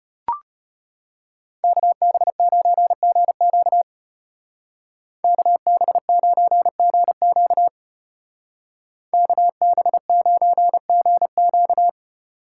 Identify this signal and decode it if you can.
Morse code, heard as a keyed tone, about 33 wpm, 700 Hz. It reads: KB9GQ KB9GQ KB9GQ